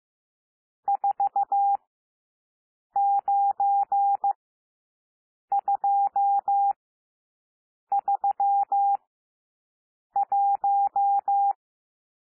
Morse code, heard as a keyed tone, about 15 wpm, spaced as for 6 wpm, 800 Hz. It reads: 49231
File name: SOUND NUMBERS 15 wpm